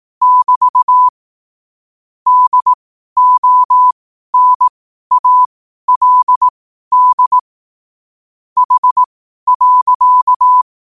Morse code, heard as a keyed tone, about 18 wpm, spaced as for 12 wpm, 1000 Hz. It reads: <BT> DONALD H.
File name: qotdfiles / 12wpm